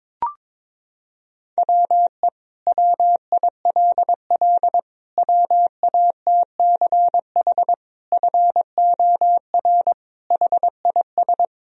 Morse code, heard as a keyed tone, about 22 wpm, 700 Hz. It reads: WE WILL WATCH FOR HIS
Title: Sentences from Top 1000 Words - Review Part 1 - 22wpm